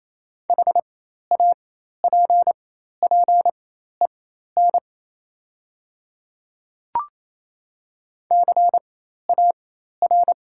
Morse code, heard as a keyed tone, about 28 wpm, spaced as for 13 wpm, 700 Hz. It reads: HAPPEN CAR
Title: 4x Character Spacing - Top 500 Words - 28wpm